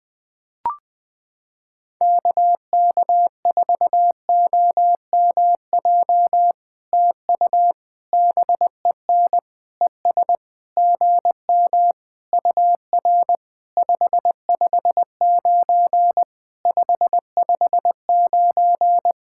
Morse code, heard as a keyed tone, about 20 wpm, 700 Hz. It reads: KK4OMJ TU BEN ES GM UR 559 559
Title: POTA QSO Practice - 20wpm